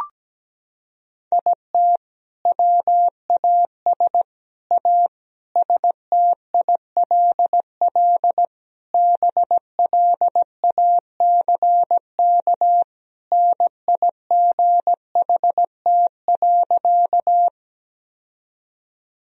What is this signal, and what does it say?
Morse code, heard as a keyed tone, about 17 wpm, 700 Hz. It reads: IT WAS A STILL BLACK NIGHT.
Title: Sentences from Top 1000 Words - Review Part 2 - 17wpm